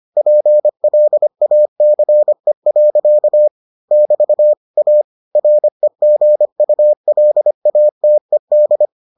Morse code, heard as a keyed tone, about 25 wpm, 600 Hz. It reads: PLACE. <BT> A REGULATED